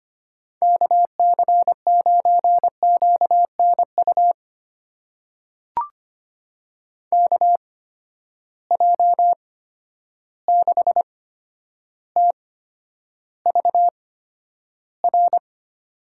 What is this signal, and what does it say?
Morse code, heard as a keyed tone, about 25 wpm, spaced as for 11 wpm, 700 Hz. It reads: KC9QNU KJ6TVR
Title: USA Call Signs - 6 Characters - 8x Character Spacing - 25wpm